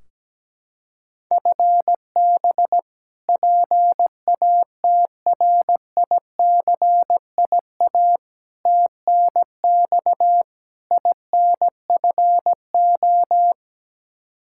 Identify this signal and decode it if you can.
Morse code, heard as a keyed tone, about 17 wpm, 700 Hz. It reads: FB PATRICIA TNX INFO
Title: Practice QSO - Exchange 3 - More Info - 17wpm